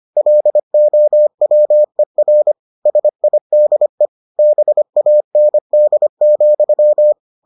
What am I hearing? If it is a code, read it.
Morse code, heard as a keyed tone, about 25 wpm, 600 Hz. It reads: LOWER SIDE BAND,